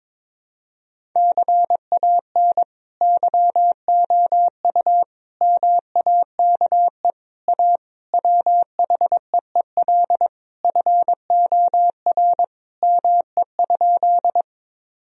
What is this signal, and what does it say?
Morse code, heard as a keyed tone, about 22 wpm, 700 Hz. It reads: CAN YOU MAKE A WHEEL FOR ME?
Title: Sentences from Top 1000 Words - Review Part 3 - 22wpm